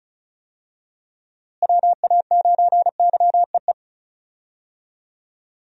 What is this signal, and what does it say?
Morse code, heard as a keyed tone, about 36 wpm, 700 Hz. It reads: WA9YEE
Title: USA Call Signs - 6 Characters - Repeated 3x - 35wpm